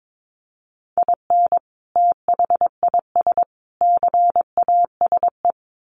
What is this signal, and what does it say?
Morse code, heard as a keyed tone, about 22 wpm, 700 Hz. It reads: IN THIS CASE
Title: Sets of 3 Words 22wpm